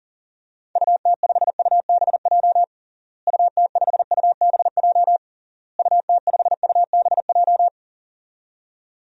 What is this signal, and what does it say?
Morse code, heard as a keyed tone, about 40 wpm, 700 Hz. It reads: UT5UBJ UT5UBJ UT5UBJ
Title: DXCC-100 Call Signs - 6 Characters - Repeated 3x - 40wpm